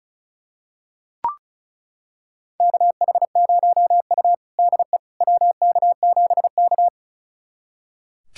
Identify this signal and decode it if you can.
Morse code, heard as a keyed tone, about 34 wpm, 700 Hz. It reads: KH0U DE WK7K
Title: USA Callsign DE Callsign - 4 Characters - 35wpm